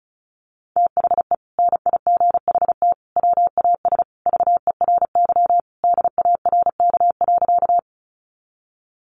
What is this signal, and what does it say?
Morse code, heard as a keyed tone, about 34 wpm, 700 Hz. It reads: THE NIGHT WAS VERY DARK.
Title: Sentences from Top 500 Words 35wpm